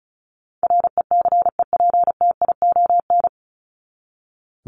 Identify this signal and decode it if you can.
Morse code, heard as a keyed tone, about 34 wpm, 700 Hz. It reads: RECEPTION
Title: Top 1751-2000 Words 35wpm